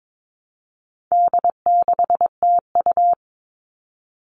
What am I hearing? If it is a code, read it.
Morse code, heard as a keyed tone, about 22 wpm, 700 Hz. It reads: D6TU
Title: All Cut Numbers - 4-Digit 22wpm